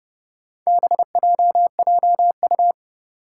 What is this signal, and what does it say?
Morse code, heard as a keyed tone, about 31 wpm, 700 Hz. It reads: BJJU
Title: Four Challenging Letter-Number Combinations 30wpm